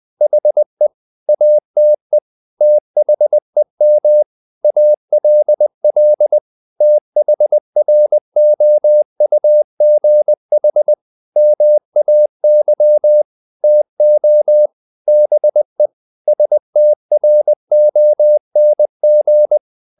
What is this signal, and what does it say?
Morse code, heard as a keyed tone, about 20 wpm, 600 Hz. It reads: HE ATE THEM ALL THROUGH MAY TO BE STRONG
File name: the old man and the sea / the old man and the sea part 1 20wpm